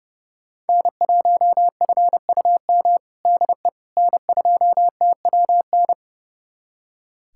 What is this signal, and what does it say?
Morse code, heard as a keyed tone, about 29 wpm, 700 Hz. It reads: N1FUM DE N2TWN
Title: USA Callsign DE Callsign - 5 Characters - 30wpm